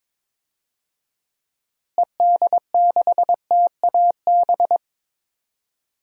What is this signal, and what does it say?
Morse code, heard as a keyed tone, about 21 wpm, 700 Hz. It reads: ED6TAB
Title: DXCC-100 Call Signs - 6 Characters - Repeated 3x - 22wpm